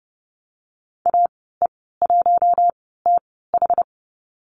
Morse code, heard as a keyed tone, about 30 wpm, spaced as for 17 wpm, 700 Hz. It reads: AE1TH
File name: Lesson 15g - Number 5 - Call Signs 30wpm